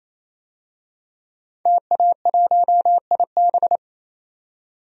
Morse code, heard as a keyed tone, about 28 wpm, 700 Hz. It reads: TA1IB
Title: DXCC-100 Call Signs - 5 Characters - 28wpm